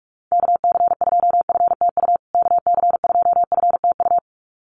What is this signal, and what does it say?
Morse code, heard as a keyed tone, about 44 wpm, 700 Hz. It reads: KC2FTU KC2FTU
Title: USA Call Signs Repeated 2x 45wpm